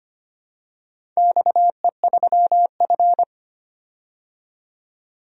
Morse code, heard as a keyed tone, about 25 wpm, 700 Hz. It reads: XE3F
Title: DXCC-100 Call Signs - 4 Characters - 25wpm